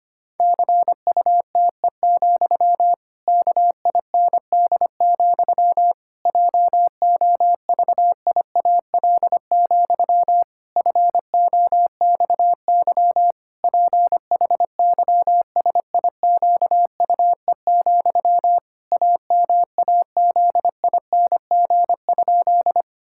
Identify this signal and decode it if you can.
Morse code, heard as a keyed tone, about 25 wpm, 700 Hz. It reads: CUTE, KIND, JOVIAL, FOXY PHYSIQUE, AMAZING?
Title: Pangram Challenge v2 - 25wpm